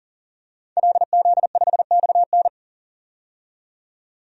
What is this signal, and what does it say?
Morse code, heard as a keyed tone, about 40 wpm, 700 Hz. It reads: LZ5XN
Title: DXCC-100 Call Signs - 5 Characters - 40wpm